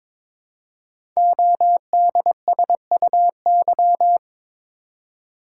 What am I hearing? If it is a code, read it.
Morse code, heard as a keyed tone, about 23 wpm, 700 Hz. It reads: ODSUY
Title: Code Groups - 5 Letters - 22wpm